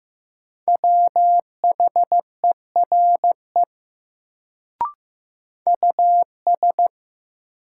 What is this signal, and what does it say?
Morse code, heard as a keyed tone, about 15 wpm, 700 Hz. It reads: WHERE US
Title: Sentences from Top 200 Words - New Word Drill - 15wpm